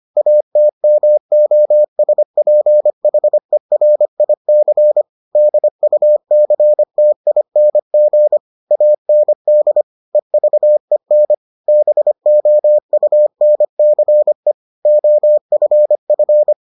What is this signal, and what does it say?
Morse code, heard as a keyed tone, about 25 wpm, 600 Hz. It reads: ATMOSPHERIC DUCTING AND EVEN BOUNCE OFF